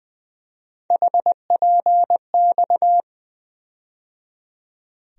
Morse code, heard as a keyed tone, about 21 wpm, 700 Hz. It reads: HPX
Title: USA Call Sign Postfixes - 3 Letters 20wpm